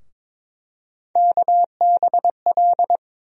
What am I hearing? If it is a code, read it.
Morse code, heard as a keyed tone, about 22 wpm, 700 Hz. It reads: KBL
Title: USA Call Sign Postfixes - 3 Letters 22wpm